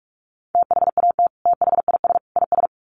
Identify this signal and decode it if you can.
Morse code, heard as a keyed tone, about 44 wpm, 700 Hz. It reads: THAT THIS IS
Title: Sets of 3 Words 45wpm